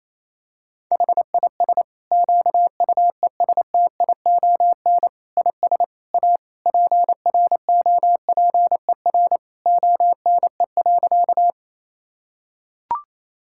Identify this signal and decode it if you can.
Morse code, heard as a keyed tone, about 27 wpm, 700 Hz. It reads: HIS QUESTION IS A PROPER ONE.
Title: Sentences from Top 1000 Words - Review Part 5 - 28wpm